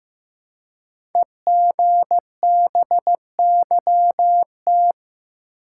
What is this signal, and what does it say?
Morse code, heard as a keyed tone, about 15 wpm, 700 Hz. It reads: EGBYT
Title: Code Groups - 5 Letter:Numbers - 15wpm